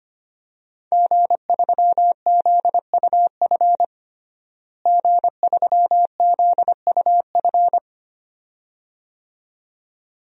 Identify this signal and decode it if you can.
Morse code, heard as a keyed tone, about 25 wpm, 700 Hz. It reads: G3ZUF G3ZUF
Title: DXCC-100 Call Signs - 5 Characters - Repeated 2x - 25wpm